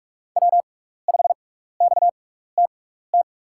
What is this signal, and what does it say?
Morse code, heard as a keyed tone, about 44 wpm, spaced as for 15 wpm, 700 Hz. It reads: W5XTT
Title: USA Call Signs - 6x Character Spacing - 45wpm